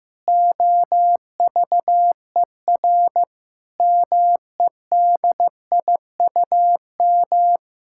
Morse code, heard as a keyed tone, about 16 wpm, 700 Hz. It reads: OVER MEDIUM
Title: Sentences from Top 1000 Words - Review Part 6 - 15wpm